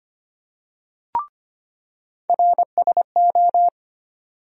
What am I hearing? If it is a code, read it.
Morse code, heard as a keyed tone, about 25 wpm, 700 Hz. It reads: RSO
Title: USA Call Sign Postfixes - 3 Letters 25wpm